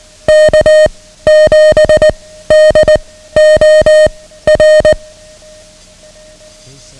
Morse code, heard as a keyed tone, about 19 wpm, 620 Hz. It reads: K 7 D O R
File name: ham radio / CodeQuick / Code Quick 09